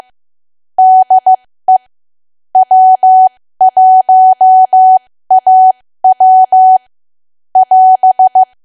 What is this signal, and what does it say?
Morse code, heard as a keyed tone, about 15 wpm, spaced as for 13 wpm, 750 Hz. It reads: DE W1AW <AS>
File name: cwa / W1AW-Level2 / 13WPM